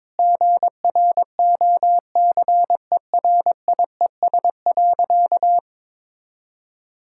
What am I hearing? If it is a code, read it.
Morse code, heard as a keyed tone, about 21 wpm, 700 Hz. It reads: GROCERIES.
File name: practice / 22wpm / Sentences with 3-Word Phrases - 22wpm